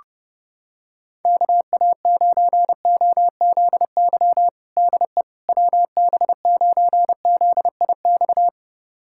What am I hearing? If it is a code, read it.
Morse code, heard as a keyed tone, about 28 wpm, 700 Hz. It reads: KA9OZY DE WB9ZIX
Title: USA Callsign DE Callsign - 6 Characters - 30wpm